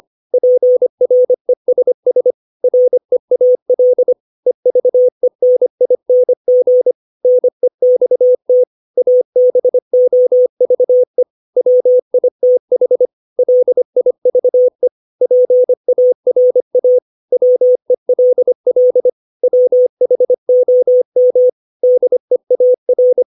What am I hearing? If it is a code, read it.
Morse code, heard as a keyed tone, about 25 wpm, 500 Hz. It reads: PRESS REAL EVENING NEXT ABOVE WITH LIVE PARA WELL WHOM DEAR
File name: 632 words / 632 Words 25WPM